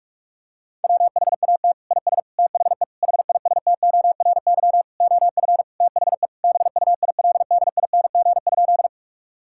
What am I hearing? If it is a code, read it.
Morse code, heard as a keyed tone, about 46 wpm, 700 Hz. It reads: WHAT IS THE HISTORY OF THE BUILDING?
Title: Sentences from Top 1000 Words - Review Part 5 - 45wpm